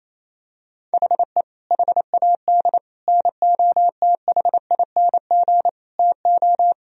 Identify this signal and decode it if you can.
Morse code, heard as a keyed tone, about 28 wpm, 700 Hz. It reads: HE HAD NOTHING TO
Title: Sentences from Top 1000 Words - Review Part 3 - 28wpm